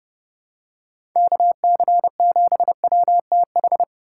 Morse code, heard as a keyed tone, about 32 wpm, 700 Hz. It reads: KC7WTH
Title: USA Call Signs - 6 Characters - 2x Character Spacing - 30wpm